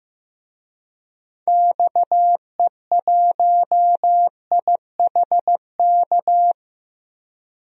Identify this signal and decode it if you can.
Morse code, heard as a keyed tone, about 15 wpm, 700 Hz. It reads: XE1IHK
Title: DXCC-100 Call Signs - 6 Characters - 15wpm